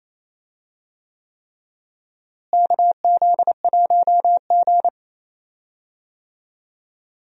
Morse code, heard as a keyed tone, about 28 wpm, 700 Hz. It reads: KZ1G
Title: USA Call Signs - 4 Characters - Repeated 3x - 28wpm